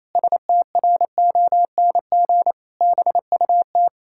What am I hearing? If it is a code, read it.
Morse code, heard as a keyed tone, about 28 wpm, 700 Hz. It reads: STRONG BUT
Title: Pangram Challenge v2 - 28wpm